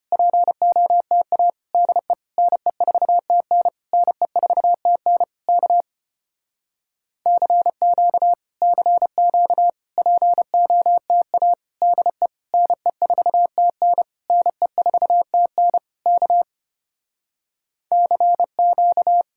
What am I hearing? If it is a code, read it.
Morse code, heard as a keyed tone, about 30 wpm, 700 Hz. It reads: POTA DE NE4TN NE4TN K CQ CQ POTA DE NE4TN NE4TN K CQ
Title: Speed-Racing Common QSO Phrases Encore - 25wpm